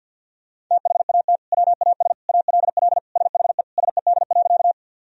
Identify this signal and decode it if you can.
Morse code, heard as a keyed tone, about 48 wpm, 700 Hz. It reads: THAT WAS ALL SHE SED.
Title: Sentences from Top 100 Words and Abbreviations 50wpm